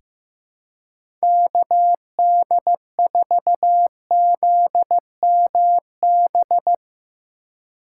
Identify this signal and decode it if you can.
Morse code, heard as a keyed tone, about 16 wpm, 700 Hz. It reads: KD4ZMB
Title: USA Call Signs - 6 Characters - 3x Character Spacing - 15wpm